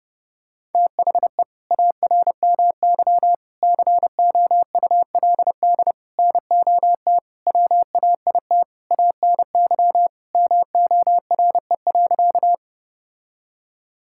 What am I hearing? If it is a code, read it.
Morse code, heard as a keyed tone, about 29 wpm, 700 Hz. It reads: THE ARMY COULD NOT WAIT ANY MORE.
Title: Sentences from Top 1000 Words - Review Part 5 - 30wpm